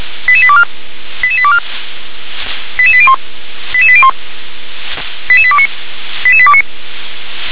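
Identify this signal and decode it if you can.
Five-tone selective call (ZVEI), heard as quick runs of five tones, tones of about 70 ms. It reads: 88824 88824 88801 88801 88829 88829
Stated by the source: Five-tone ZVEI selective call 88771 received by SDR.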